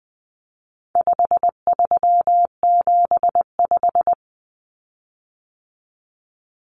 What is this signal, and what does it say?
Morse code, heard as a keyed tone, about 20 wpm, 700 Hz. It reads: 5375
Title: Common Cut Numbers - 4-Digit 20wpm